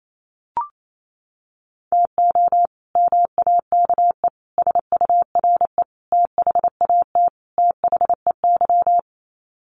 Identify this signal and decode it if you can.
Morse code, heard as a keyed tone, about 28 wpm, 700 Hz. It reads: TO MAKE SURE THAT THEY
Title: Sets of 5 Words Encore2 28wpm